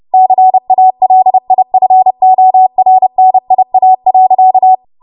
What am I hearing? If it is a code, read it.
Morse code, heard as a keyed tone, about 30 wpm, 750 Hz. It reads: CALIFORNIA.
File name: Morse / 30WPM